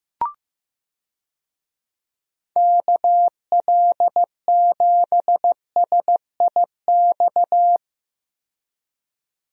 Morse code, heard as a keyed tone, about 15 wpm, 700 Hz. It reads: KL7SIX
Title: DXCC-100 Call Signs - 6 Characters - 15wpm